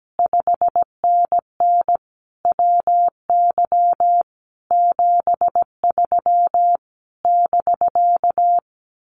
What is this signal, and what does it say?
Morse code, heard as a keyed tone, about 17 wpm, 700 Hz. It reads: 5NN WY 73 <BK>
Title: Common QSO Phrases Encore 17wpm